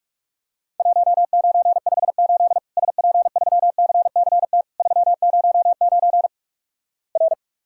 Morse code, heard as a keyed tone, about 46 wpm, 700 Hz. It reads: 1958 SP3CCT 309 R
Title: Worked All Europe DX Contest - QTC Exchange - 45wpm